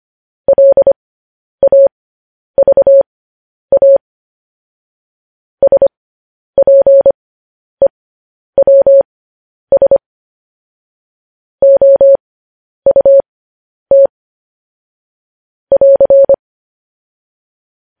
Morse code, heard as a keyed tone, about 25 wpm, spaced as for 10 wpm, 550 Hz. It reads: LAVA SPEWS OUT <AR>
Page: news headlines in morse code at 10 wpm